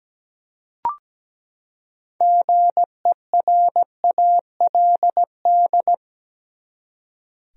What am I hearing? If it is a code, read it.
Morse code, heard as a keyed tone, about 17 wpm, 700 Hz. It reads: GERALD
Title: USA Common Names - 17wpm